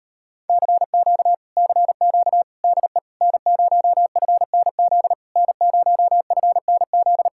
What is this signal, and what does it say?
Morse code, heard as a keyed tone, about 33 wpm, 700 Hz. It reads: CQ CQ DE N0FNZ N0FNZ